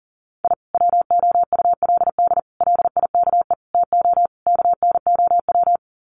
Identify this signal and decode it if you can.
Morse code, heard as a keyed tone, about 40 wpm, 700 Hz. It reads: I WOULD LIKE TO KNOW
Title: Sets of 5 Words Encore 40wpm